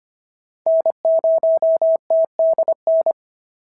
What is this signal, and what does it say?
Morse code, heard as a keyed tone, about 23 wpm, 650 Hz. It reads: N0TDN
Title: SOTA QSO Practice 25wpm